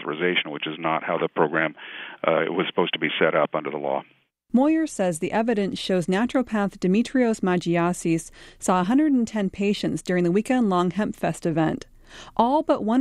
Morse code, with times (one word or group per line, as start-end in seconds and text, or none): none